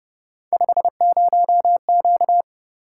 Morse code, heard as a keyed tone, about 31 wpm, 700 Hz. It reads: 50Q
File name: Speed-Racing - Three Letter-Number Combinations - 25wpm